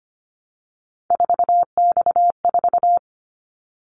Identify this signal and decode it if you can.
Morse code, heard as a keyed tone, about 25 wpm, 700 Hz. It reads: 4X4
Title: Callsign Prefixes for Top 50 DXCC - 25wpm